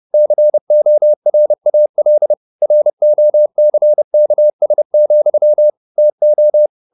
Morse code, heard as a keyed tone, about 30 wpm, 600 Hz. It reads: CORAL ROCKS, TO
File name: the old man and the sea / the old man and the sea 1 30wpm